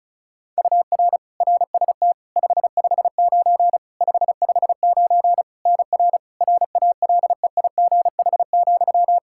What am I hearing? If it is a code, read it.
Morse code, heard as a keyed tone, about 36 wpm, 700 Hz. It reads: UR RST 559 559 NR RALEIGH,
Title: K1USN SST QSO Practice 35wpm